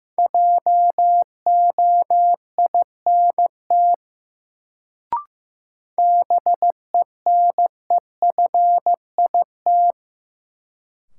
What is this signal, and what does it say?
Morse code, heard as a keyed tone, about 15 wpm, 700 Hz. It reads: JOINT BENEFIT
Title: Sets of 5 Words Encore - New Word Drill - 15wpm